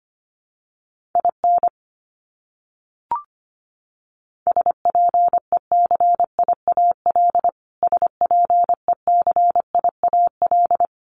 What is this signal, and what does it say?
Morse code, heard as a keyed tone, about 25 wpm, 700 Hz. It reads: IN SPECIAL SPECIAL
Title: Top 500 Words Repeated 25wpm